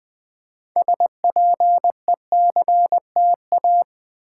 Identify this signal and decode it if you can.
Morse code, heard as a keyed tone, about 20 wpm, 700 Hz. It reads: SPECTA
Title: Build A Word - 3 Letters At A Time - 9-Letter Words - 20wpm